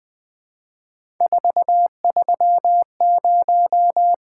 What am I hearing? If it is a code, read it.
Morse code, heard as a keyed tone, about 21 wpm, 700 Hz. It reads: 430
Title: Three Letter-Number Combinations - 20wpm